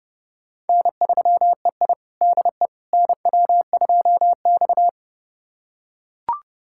Morse code, heard as a keyed tone, about 30 wpm, 700 Hz. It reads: N3EI DE NW2X
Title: USA Callsign DE Callsign - 4 Characters - 30wpm